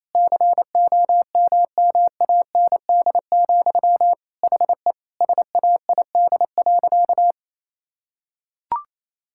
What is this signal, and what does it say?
Morse code, heard as a keyed tone, about 28 wpm, 700 Hz. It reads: COMMAND, HE SAID.
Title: Sentences from Top 1600 Words 28wpm